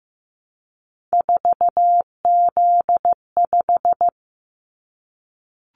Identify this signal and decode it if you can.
Morse code, heard as a keyed tone, about 15 wpm, 700 Hz. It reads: 4Z5
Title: Callsign Prefixes for Top 50 DXCC - 15wpm